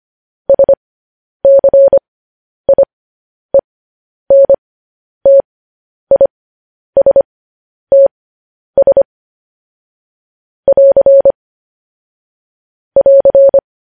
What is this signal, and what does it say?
Morse code, heard as a keyed tone, about 25 wpm, spaced as for 10 wpm, 550 Hz. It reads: SCIENTISTS <AR> <AR>
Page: news headlines in morse code at 10 wpm